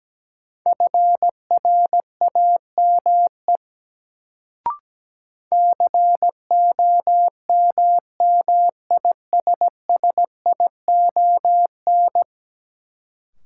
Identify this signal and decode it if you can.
Morse code, heard as a keyed tone, about 17 wpm, 700 Hz. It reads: FRAME COMMISSION
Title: Sentences from Top 1400 Words - New Word Drill Alt - 17wpm